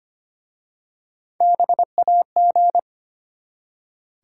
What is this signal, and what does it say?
Morse code, heard as a keyed tone, about 25 wpm, 700 Hz. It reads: BAG
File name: practice / 25wpm / Easy Words with Hints - 25wpm